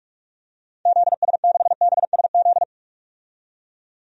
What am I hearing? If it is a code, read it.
Morse code, heard as a keyed tone, about 46 wpm, 700 Hz. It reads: ZS6BSZ
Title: DXCC-100 Call Signs - 6 Characters - 45wpm